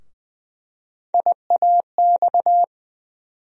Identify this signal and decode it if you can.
Morse code, heard as a keyed tone, about 20 wpm, 700 Hz. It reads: IAX
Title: USA Call Sign Postfixes - 3 Letters 20wpm